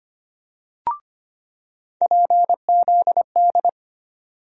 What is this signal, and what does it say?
Morse code, heard as a keyed tone, about 25 wpm, 700 Hz. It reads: PZD
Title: USA Call Sign Postfixes - 3 Letters 25wpm